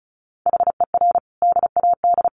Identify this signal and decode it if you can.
Morse code, heard as a keyed tone, about 34 wpm, 700 Hz. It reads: HER DAD
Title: Additional Easy Phrases 35wpm